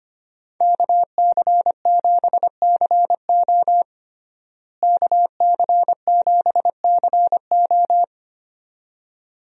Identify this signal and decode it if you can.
Morse code, heard as a keyed tone, about 25 wpm, 700 Hz. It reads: KC7CO KC7CO
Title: USA Call Signs - 5 Characters - Repeated 2x - 25wpm